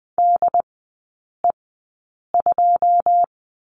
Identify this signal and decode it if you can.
Morse code, heard as a keyed tone, about 20 wpm, spaced as for 9 wpm, 690 Hz. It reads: DE2
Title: 145 Lesson 24e - Letter y - Three Characters 20wpm